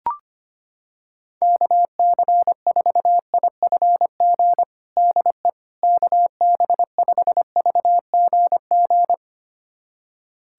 Morse code, heard as a keyed tone, about 25 wpm, 700 Hz. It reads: KC4IFG DE KB5VGG
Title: USA Callsign DE Callsign - 6 Characters - 25wpm